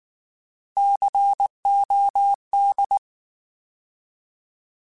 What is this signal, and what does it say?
Morse code, heard as a keyed tone, about 19 wpm, 800 Hz. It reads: COD
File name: Alphabetical / Alphabetical mp3s / Cs @18